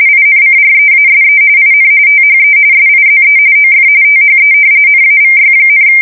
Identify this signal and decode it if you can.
RTTY, about 45.45 baud. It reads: INCREASE IN THE SIZE OF THE NAVY,
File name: fsk170 / usnavy4